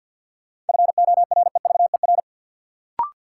Morse code, heard as a keyed tone, about 53 wpm, 700 Hz. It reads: FOREVER
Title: Sentences from Top 1600 Words - New Word Drill - 50wpm